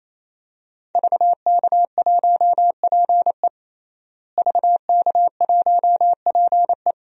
Morse code, heard as a keyed tone, about 27 wpm, 700 Hz. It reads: VK1PE VK1PE
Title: DXCC-100 Call Signs - 5 Characters - Repeated 3x - 28wpm